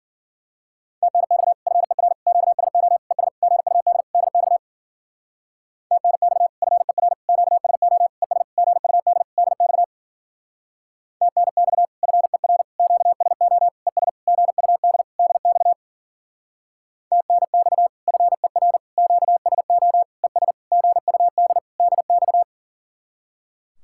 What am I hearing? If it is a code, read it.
Morse code, heard as a keyed tone, about 50 wpm, 700 Hz. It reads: TNX FER QSO ES GUD DX TNX FER QSO ES GUD DX TNX FER QSO ES GUD DX TNX FER QSO ES GUD DX